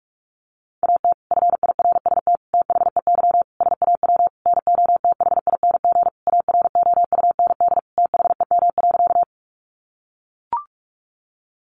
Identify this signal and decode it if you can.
Morse code, heard as a keyed tone, about 44 wpm, 700 Hz. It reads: AT FIRST THEY SAW NOTHING AROUND THEM.
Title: Sentences from Top 600 Words 45wpm